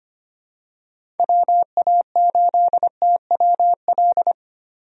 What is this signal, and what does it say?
Morse code, heard as a keyed tone, about 25 wpm, 700 Hz. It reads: WA8TWL
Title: USA Call Signs - 6 Characters - Repeated 3x - 25wpm